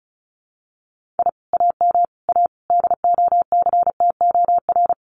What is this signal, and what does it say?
Morse code, heard as a keyed tone, about 34 wpm, 700 Hz. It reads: I AM A DOCTOR
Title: Common QSO Phrases Encore 35wpm